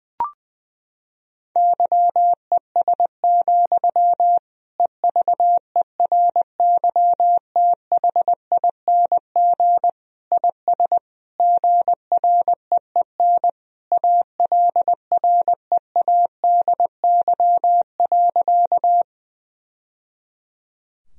Morse code, heard as a keyed tone, about 20 wpm, 700 Hz. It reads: YES, EVERYTHING IS GREEN ALREADY.
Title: Sentences from Top 1000 Words - Review Part 2 - 20wpm